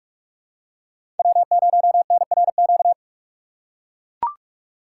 Morse code, heard as a keyed tone, about 42 wpm, 700 Hz. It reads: W0NRQ
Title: USA Call Signs - 5 Characters - 6x Character Spacing - 45wpm